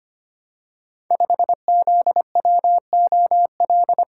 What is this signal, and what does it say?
Morse code, heard as a keyed tone, about 25 wpm, 700 Hz. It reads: 5ZWOL
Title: Code Groups - 5 Letter:Numbers - 25wpm